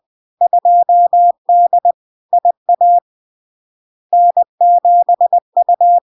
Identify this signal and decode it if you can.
Morse code, heard as a keyed tone, about 20 wpm, 700 Hz. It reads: 2D IA N7U
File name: LARK-FD-2022-Log-in-CW-20wpm-700-Hz